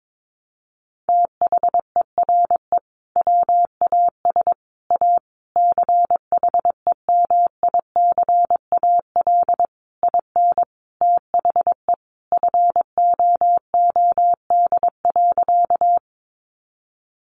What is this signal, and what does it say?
Morse code, heard as a keyed tone, about 22 wpm, 700 Hz. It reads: THERE WAS A CHEMICAL IN THE FOOD.
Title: Sentences from Top 1200 Words 22wpm